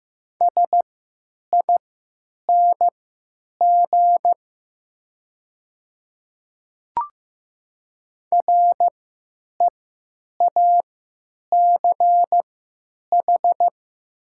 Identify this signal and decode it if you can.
Morse code, heard as a keyed tone, about 15 wpm, spaced as for 9 wpm, 700 Hz. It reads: SING REACH
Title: 3x Character Spacing - Top 500 Words - 15wpm